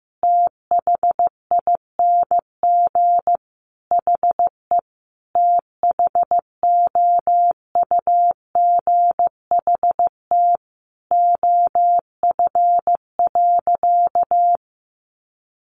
Morse code, heard as a keyed tone, about 15 wpm, 700 Hz. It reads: THING HE THOUGHT OF.
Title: Sentences from Top 300 Words 15wpm